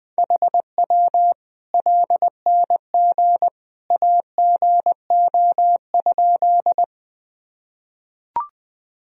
Morse code, heard as a keyed tone, about 20 wpm, 700 Hz. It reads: HW LNG AGO?
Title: Sentences from Top 500 Words and Abbreviations 20wpm